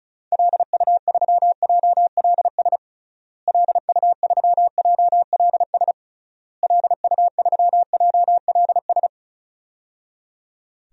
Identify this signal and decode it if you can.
Morse code, heard as a keyed tone, about 36 wpm, 700 Hz. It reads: LU3JLS LU3JLS LU3JLS
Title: DXCC-100 Call Signs - 6 Characters - Repeated 3x - 35wpm